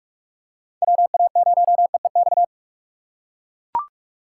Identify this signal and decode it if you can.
Morse code, heard as a keyed tone, about 41 wpm, 700 Hz. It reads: WA0EEX
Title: USA Call Signs - 6 Characters - 4x Character Spacing - 45wpm